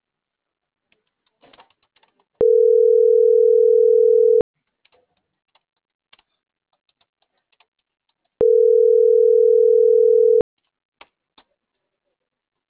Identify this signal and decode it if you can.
Ringback tone (North America), 440 + 480 Hz.